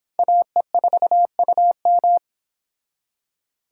Morse code, heard as a keyed tone, about 28 wpm, 700 Hz. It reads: AE4UM